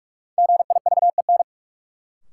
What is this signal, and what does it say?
Morse code, heard as a keyed tone, about 46 wpm, 700 Hz. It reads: GIVEN
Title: Sentences from Top 1400 Words - New Word Drill - 45wpm